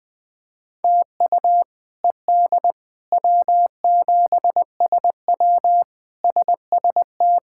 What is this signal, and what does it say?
Morse code, heard as a keyed tone, about 20 wpm, 700 Hz. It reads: TU ED W7SW SST
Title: K1USN SST QSO Practice - 20wpm